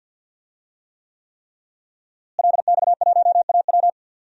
Morse code, heard as a keyed tone, about 46 wpm, 700 Hz. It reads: LX1AW
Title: DXCC-100 Call Signs - 5 Characters - Repeated 3x - 50wpm